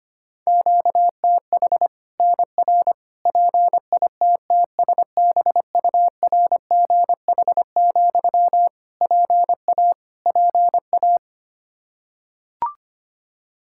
Morse code, heard as a keyed tone, about 25 wpm, 700 Hz. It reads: QTH NR PITTSBURGH, PA PA
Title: Practice QSO - Exchange 1 - Beginning - 25wpm